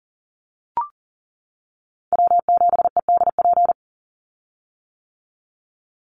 Morse code, heard as a keyed tone, about 40 wpm, 700 Hz. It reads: W7EDP
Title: USA Call Signs 40wpm